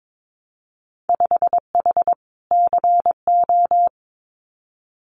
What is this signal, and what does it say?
Morse code, heard as a keyed tone, about 22 wpm, 700 Hz. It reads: 5H CO
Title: Winter Field Day Exchanges Encore 22wpm